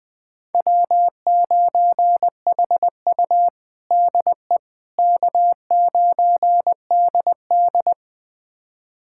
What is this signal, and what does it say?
Morse code, heard as a keyed tone, about 20 wpm, 700 Hz. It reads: W9HU DE K9DD
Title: USA Callsign DE Callsign - 4 Characters - 20wpm